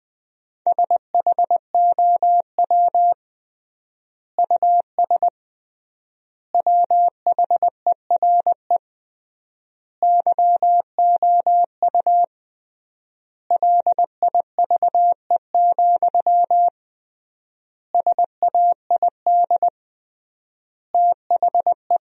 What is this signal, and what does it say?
Morse code, heard as a keyed tone, about 20 wpm, 700 Hz. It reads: SHOW US WHERE YOU LIVE, SAID THE
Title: 3x Word-Spacing - Sentences from Top 200 Words - 20wpm